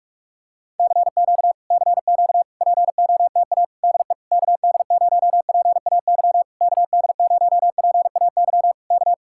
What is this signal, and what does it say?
Morse code, heard as a keyed tone, about 48 wpm, 700 Hz. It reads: CQ CQ POTA DE KD0PAY KD0PAY K